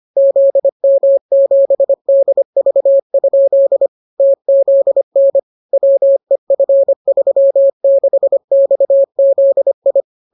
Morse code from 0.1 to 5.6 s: ZM7DV? TZN